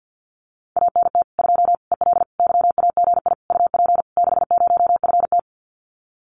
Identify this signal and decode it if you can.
Morse code, heard as a keyed tone, about 50 wpm, 700 Hz. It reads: ANT 2 EL YAGI UP 60FT
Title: Common QSO Phrases Encore 50wpm